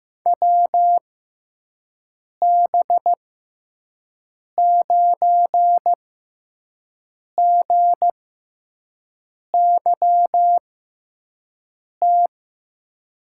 Morse code, heard as a keyed tone, about 16 wpm, spaced as for 5 wpm, 700 Hz. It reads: WB9GYT
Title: USA Call Signs - 6x Character Spacing - 15wpm